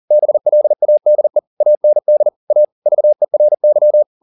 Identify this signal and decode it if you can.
Morse code, heard as a keyed tone, about 40 wpm, 600 Hz. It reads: BLADE AND A VERY
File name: the old man and the sea 2 40wpm